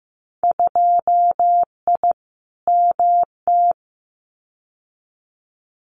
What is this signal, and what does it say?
Morse code, heard as a keyed tone, about 15 wpm, 700 Hz. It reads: 2I MT
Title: Winter Field Day Exchanges Encore 15wpm